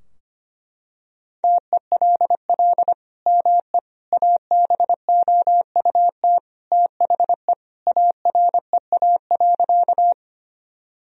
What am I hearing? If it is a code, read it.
Morse code, heard as a keyed tone, about 25 wpm, 700 Hz. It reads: TELL ME ABOUT THE AREA.
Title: Sentences from Top 1000 Words - Review Part 2 - 25wpm